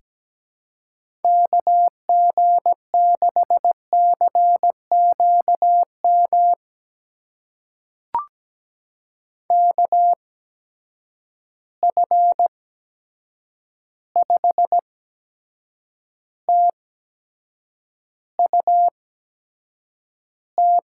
Morse code, heard as a keyed tone, about 17 wpm, spaced as for 7 wpm, 700 Hz. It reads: KG6CQM KF5TUT
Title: USA Call Signs - 8x Character Spacing - 17wpm